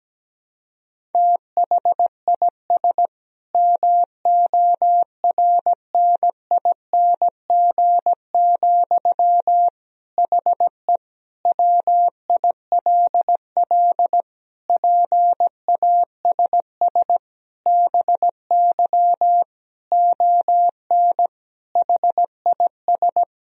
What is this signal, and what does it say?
Morse code, heard as a keyed tone, about 17 wpm, 700 Hz. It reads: THIS MORNING, HE WILL PASS BY ON HIS
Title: Sentences from Top 1000 Words - Review Part 1 - 17wpm